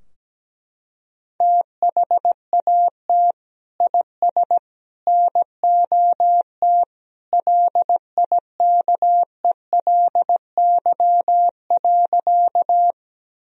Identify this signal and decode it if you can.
Morse code, heard as a keyed tone, about 17 wpm, 700 Hz. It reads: THAT IS NOT LIKELY.
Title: Sentences from Top 1000 Words - Review Part 4 - 17wpm